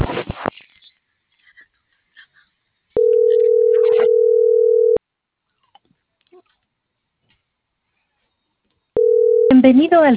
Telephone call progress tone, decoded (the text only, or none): ringback tone